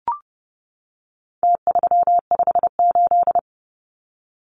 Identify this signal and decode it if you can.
Morse code, heard as a keyed tone, about 30 wpm, 700 Hz. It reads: T358
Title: Numbers - Contest Style - 4-Digit 30wpm